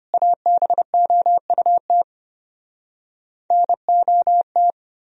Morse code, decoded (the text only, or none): ABOUT NOT